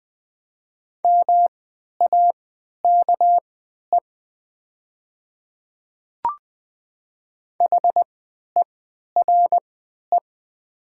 Morse code, heard as a keyed tone, about 20 wpm, spaced as for 11 wpm, 700 Hz. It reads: MAKE HERE
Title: 3x Character Spacing - Top 200 Words - 20wpm